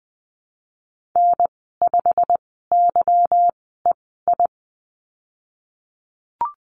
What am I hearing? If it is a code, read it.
Morse code, heard as a keyed tone, about 20 wpm, spaced as for 15 wpm, 700 Hz. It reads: N5YEI
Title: 173 Lesson 28g - Number 9 - Call Signs 20wpm